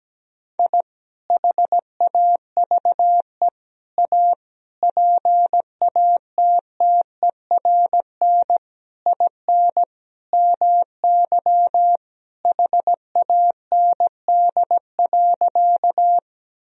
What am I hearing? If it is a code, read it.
Morse code, heard as a keyed tone, about 17 wpm, 700 Hz. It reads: I HAVE A PATTERN IN MY HAND.
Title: Sentences from Top 1000 Words - Review Part 2 - 17wpm